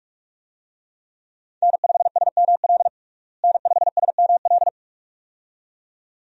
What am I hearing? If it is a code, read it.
Morse code, heard as a keyed tone, about 46 wpm, 700 Hz. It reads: N5SML N5SML
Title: USA Call Signs - 5 Characters - Repeated 2x - 45wpm